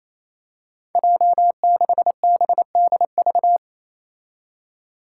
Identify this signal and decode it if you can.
Morse code, heard as a keyed tone, about 28 wpm, 700 Hz. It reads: J6BDV
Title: Five Challenging Letter-Number Combinations 28wpm